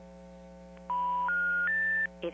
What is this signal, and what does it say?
Special information tone.
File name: TurtleCreekEWSD